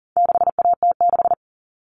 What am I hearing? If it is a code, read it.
Morse code, heard as a keyed tone, about 40 wpm, 700 Hz. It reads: 6AT6
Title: All Cut Numbers - 4-Digit 40wpm